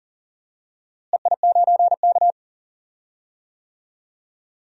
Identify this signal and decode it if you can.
Morse code, heard as a keyed tone, about 40 wpm, 700 Hz. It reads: EI9K
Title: DXCC-100 Call Signs - 4 Characters - Repeated 3x - 40wpm